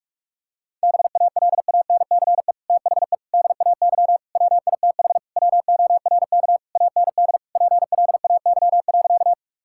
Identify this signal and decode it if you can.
Morse code, heard as a keyed tone, about 41 wpm, 700 Hz. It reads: BALANCE THE DAY WITH WORK AND PLAY.
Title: Sentences from Top 1000 Words - Review Part 6 - 45wpm